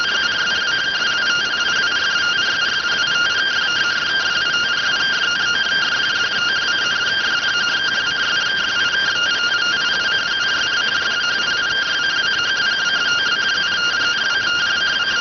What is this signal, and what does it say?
Two-tone teleprinter signal, about 170 Hz shift.
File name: findings / CT 205 ax mips, iDose 3 0006.gen-42 rx